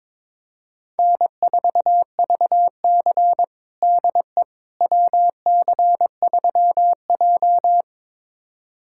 Morse code, heard as a keyed tone, about 22 wpm, 700 Hz. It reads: N4VC DE WC3J
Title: USA Callsign DE Callsign - 4 Characters - 22wpm